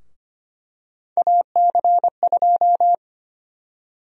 Morse code, heard as a keyed tone, about 25 wpm, 700 Hz. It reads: AC2
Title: USA Call Sign Prefixes - 25wpm